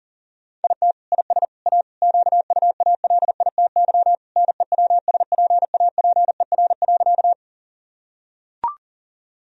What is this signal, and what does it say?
Morse code, heard as a keyed tone, about 40 wpm, 700 Hz. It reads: IT IS A QUALITY NEWSPAPER.
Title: Sentences from Top 1000 Words - Review Part 4 - 40wpm